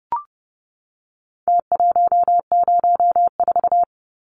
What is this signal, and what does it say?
Morse code, heard as a keyed tone, about 30 wpm, 700 Hz. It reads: T104
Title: Numbers - Contest Style - 4-Digit 30wpm